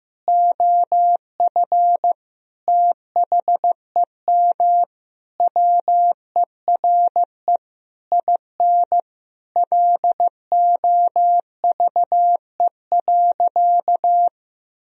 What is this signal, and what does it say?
Morse code, heard as a keyed tone, about 16 wpm, 700 Hz. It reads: OF THEM WERE IN LOVE.
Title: Sentences from Top 1000 Words - Review Part 4 - 15wpm